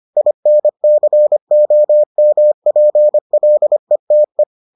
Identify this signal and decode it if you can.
Morse code, heard as a keyed tone, about 25 wpm, 600 Hz. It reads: INCOMPLETE